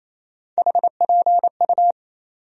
Morse code, heard as a keyed tone, about 28 wpm, 700 Hz. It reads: HPU
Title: USA Call Sign Postfixes - 3 Letters 28wpm